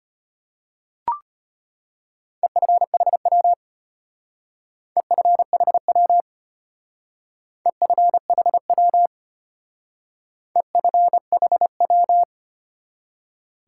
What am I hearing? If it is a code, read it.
Morse code, heard as a keyed tone, about 31 wpm, 700 Hz. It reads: EFHW EFHW EFHW EFHW
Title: Speed-Racing Top 180 QSO Elements 25wpm